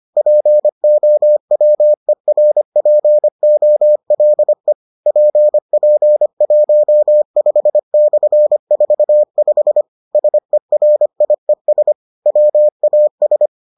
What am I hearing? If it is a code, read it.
Morse code, heard as a keyed tone, about 25 wpm, 600 Hz. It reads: POWERPOLE PP15/45 SERIES WAS